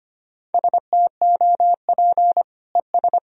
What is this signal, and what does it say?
Morse code, heard as a keyed tone, about 25 wpm, 700 Hz. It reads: STOP ES
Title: Sentences from Top 300 Words and Abbreviations 25wpm